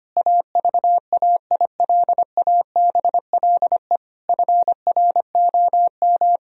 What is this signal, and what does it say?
Morse code, heard as a keyed tone, about 25 wpm, 700 Hz. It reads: AVAILABLE FROM
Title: Sentences from Top 1000 Words - Review Part 3 - 25wpm